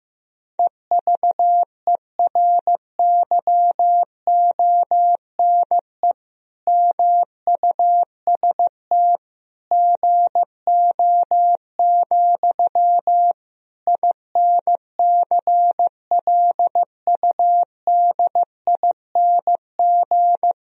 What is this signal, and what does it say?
Morse code, heard as a keyed tone, about 16 wpm, 700 Hz. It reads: EVERYONE MUST GO, INCLUDING
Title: Sentences from Top 1000 Words - Review Part 4 - 15wpm